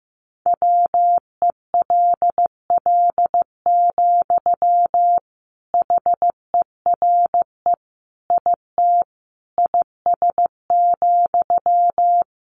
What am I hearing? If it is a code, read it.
Morse code, heard as a keyed tone, about 15 wpm, 700 Hz. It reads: WELL, HERE IT IS,
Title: Sentences from Top 1400 Words 15wpm